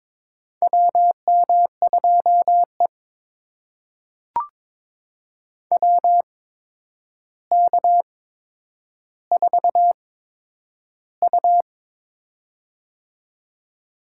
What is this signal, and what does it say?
Morse code, heard as a keyed tone, about 22 wpm, spaced as for 9 wpm, 700 Hz. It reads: WM2E WK4U
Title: USA Call Signs - 4 Characters - 8x Character Spacing - 22wpm